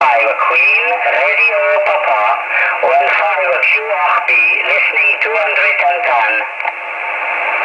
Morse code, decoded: none